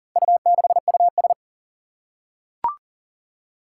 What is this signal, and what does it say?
Morse code, heard as a keyed tone, about 40 wpm, 700 Hz. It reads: U6US
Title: Four Challenging Letter-Number Combinations 40wpm